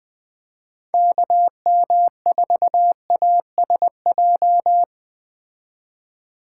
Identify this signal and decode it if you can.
Morse code, heard as a keyed tone, about 19 wpm, 700 Hz. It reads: KM4ASJ